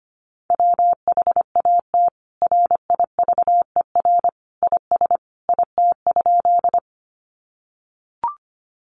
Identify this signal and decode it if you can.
Morse code, heard as a keyed tone, about 25 wpm, 700 Hz. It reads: WHAT RIVER IS IT?
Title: Sentences from Top 300 Words 25wpm